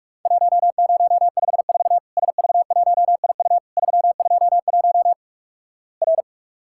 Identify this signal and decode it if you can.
Morse code, heard as a keyed tone, about 48 wpm, 700 Hz. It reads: 1054 SV1IU 321 R
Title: Worked All Europe DX Contest - QTC Exchange - 45wpm